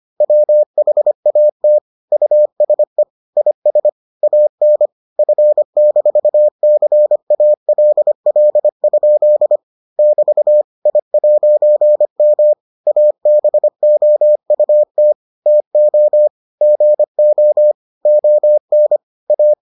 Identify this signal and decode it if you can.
Morse code, heard as a keyed tone, about 25 wpm, 600 Hz. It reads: WHAT USE IS AN F-CALL? <BT> I'M ABOUT TO GO ON A